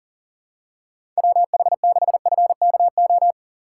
Morse code, heard as a keyed tone, about 40 wpm, 700 Hz. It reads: WH6FKO
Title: USA Call Signs - 6 Characters - 4x Character Spacing - 40wpm